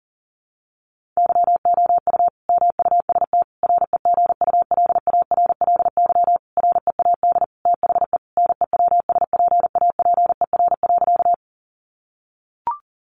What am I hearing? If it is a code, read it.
Morse code, heard as a keyed tone, about 37 wpm, 700 Hz. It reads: YOU MUST REGULARLY READ THE NEWSPAPER.
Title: Sentences from Top 1400 Words 40wpm